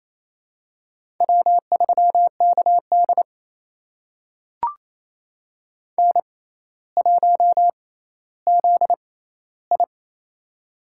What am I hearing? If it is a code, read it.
Morse code, heard as a keyed tone, about 28 wpm, spaced as for 14 wpm, 700 Hz. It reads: W3KD N1ZI
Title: USA Call Signs - 4 Characters - 6x Character Spacing - 28wpm